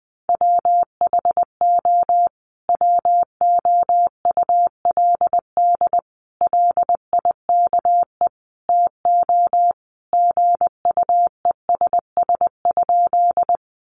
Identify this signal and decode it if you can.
Morse code, heard as a keyed tone, about 20 wpm, 700 Hz. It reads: WHO WOULD LIKE TO GUESS?
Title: Sentences from Top 1500 Words 20wpm